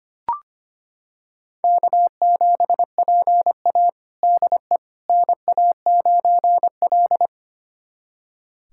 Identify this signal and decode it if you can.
Morse code, heard as a keyed tone, about 25 wpm, 700 Hz. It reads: K7PA DE NA9L
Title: USA Callsign DE Callsign - 4 Characters - 25wpm